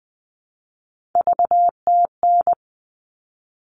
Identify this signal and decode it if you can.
Morse code, heard as a keyed tone, about 20 wpm, 700 Hz. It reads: VTN
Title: All Cut Numbers - 3-Digit 20wpm